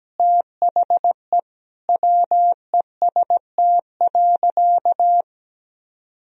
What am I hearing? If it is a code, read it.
Morse code, heard as a keyed tone, about 17 wpm, 700 Hz. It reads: THE WEST.
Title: Sentences from Top 1000 Words - Review Part 2 - 17wpm